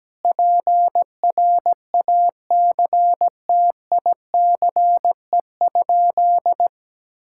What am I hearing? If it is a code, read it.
Morse code, heard as a keyed tone, about 17 wpm, 700 Hz. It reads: PRACTICE?
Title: Sentences from Top 1000 Words - Review Part 5 - 17wpm